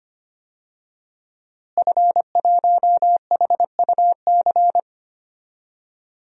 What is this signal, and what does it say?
Morse code, heard as a keyed tone, about 25 wpm, 700 Hz. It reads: F1HUC
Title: DXCC-100 Call Signs - 5 Characters - Repeated 3x - 25wpm